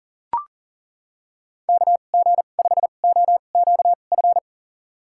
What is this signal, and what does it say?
Morse code, heard as a keyed tone, about 40 wpm, spaced as for 29 wpm, 700 Hz. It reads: KG5OQF
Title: USA Call Signs - 6 Characters - 2x Character Spacing - 40wpm